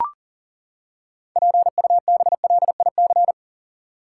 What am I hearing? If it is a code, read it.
Morse code, heard as a keyed tone, about 40 wpm, 700 Hz. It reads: PUBLIC
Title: Six Letter Words - 40wpm